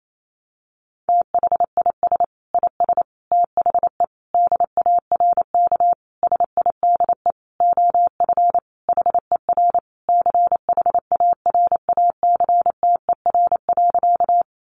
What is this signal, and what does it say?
Morse code, heard as a keyed tone, about 28 wpm, 700 Hz. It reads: THIS IS THE DARK SIDE OF HER CHARACTER.
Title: Sentences from Top 1200 Words 28wpm